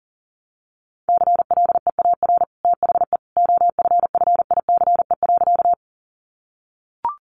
Morse code, heard as a keyed tone, about 40 wpm, 700 Hz. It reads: CLEAR THE OFFICE.
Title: Sentences from Top 1000 Words 40wpm